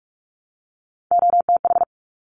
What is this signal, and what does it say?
Morse code, heard as a keyed tone, about 46 wpm, 700 Hz. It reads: OTH
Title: Top Three Letter Combinations in Words 45wpm